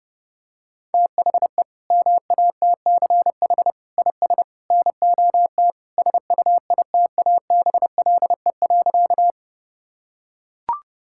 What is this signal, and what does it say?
Morse code, heard as a keyed tone, about 30 wpm, 700 Hz. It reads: THE MATCH IS NOT SUITABLE.
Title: Sentences from Top 1600 Words 30wpm